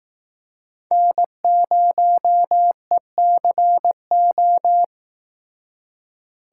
Morse code, heard as a keyed tone, about 18 wpm, 700 Hz. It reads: N0ECO
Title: Speed Racing - USA 1x3 Call Signs 15wpm